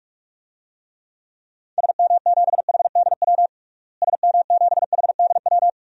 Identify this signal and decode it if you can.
Morse code, heard as a keyed tone, about 41 wpm, 700 Hz. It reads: SM7HDW SM7HDW
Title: DXCC-100 Call Signs - 6 Characters - Repeated 2x - 45wpm